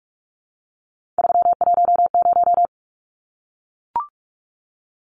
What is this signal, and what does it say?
Morse code, heard as a keyed tone, about 44 wpm, 700 Hz. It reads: 310
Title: Numbers - 3-Digit 45wpm